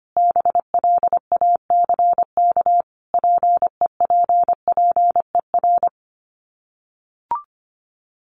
Morse code, heard as a keyed tone, about 25 wpm, 700 Hz. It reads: BLACK PEPPER
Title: Sets of 5 Words Encore 25wpm